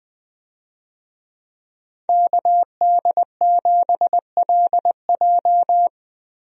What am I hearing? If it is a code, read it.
Morse code, heard as a keyed tone, about 20 wpm, 700 Hz. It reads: KD7LJ
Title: USA Call Signs - 5 Characters - Repeated 3x - 20wpm